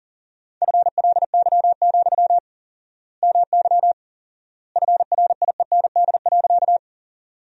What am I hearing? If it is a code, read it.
Morse code, heard as a keyed tone, about 40 wpm, 700 Hz. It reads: FLY, MY FRIEND.
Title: 4x Word-Spacing - Sentences from Top 500 Words - 40wpm